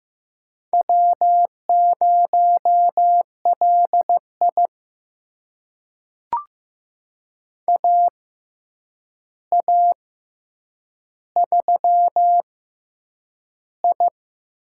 Morse code, heard as a keyed tone, about 15 wpm, spaced as for 8 wpm, 700 Hz. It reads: W0LI AA3I
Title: USA Call Signs - 4 Characters - 6x Character Spacing - 15wpm